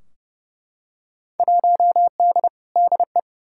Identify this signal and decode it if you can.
Morse code, heard as a keyed tone, about 30 wpm, 700 Hz. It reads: 1D DE
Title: Field Day Exchanges - 30wpm